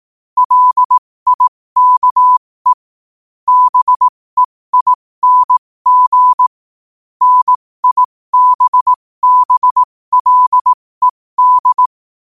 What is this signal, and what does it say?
Morse code, heard as a keyed tone, about 18 wpm, spaced as for 15 wpm, 1000 Hz. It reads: LIKE BEING NIBBLED
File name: qotdfiles / 15wpm